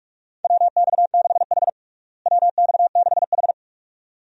Morse code, heard as a keyed tone, about 46 wpm, 700 Hz. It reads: WX6H WX6H
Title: USA Call Signs - 4 Characters - Repeated 2x - 45wpm